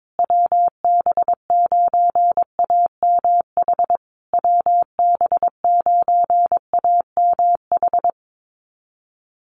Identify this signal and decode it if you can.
Morse code, heard as a keyed tone, about 22 wpm, 700 Hz. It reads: WB9AMH WB9AMH
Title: USA Call Signs Repeated 3x 22wpm